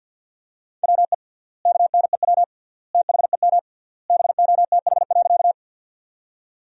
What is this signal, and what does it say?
Morse code, heard as a keyed tone, about 50 wpm, 700 Hz. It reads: WE KNEW THEM BOTH.
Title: 3x Word-Spacing - Sentences from Top 400 Words - 50wpm